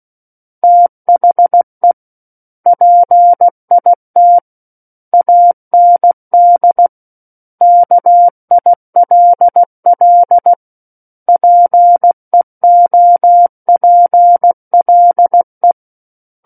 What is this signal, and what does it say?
Morse code, heard as a keyed tone, about 16 wpm, 700 Hz. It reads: THE PIT AND KILL PEOPLE
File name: Book One / 07-Chapter 7 - 16 WPM